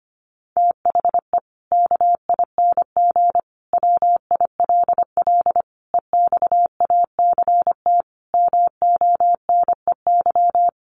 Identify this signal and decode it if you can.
Morse code, heard as a keyed tone, about 25 wpm, 700 Hz. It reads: THE KING WILL EXACT MONEY
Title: Sentences from Top 1300 Words 25wpm